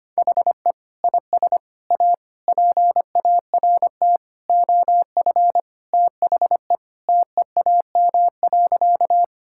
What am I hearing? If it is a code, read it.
Morse code, heard as a keyed tone, about 25 wpm, 700 Hz. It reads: HE IS A PART OF THE TEAM.
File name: practice / 25wpm / Sentences with 4-Word Phrases - 25wpm